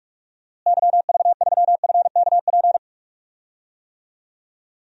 Morse code, heard as a keyed tone, about 43 wpm, 700 Hz. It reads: YV3FKP
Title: DXCC-100 Call Signs - 6 Characters - Repeated 2x - 45wpm